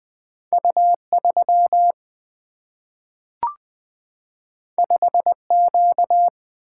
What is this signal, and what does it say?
Morse code, heard as a keyed tone, about 20 wpm, 700 Hz. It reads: U 3 5 Q
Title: Single-Letter Single-Number 20wpm